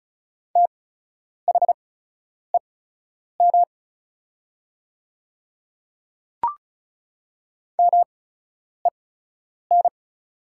Morse code, heard as a keyed tone, about 36 wpm, spaced as for 10 wpm, 700 Hz. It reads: THEM MEN
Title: 8x Character Spacing - Top 200 Words - 35wpm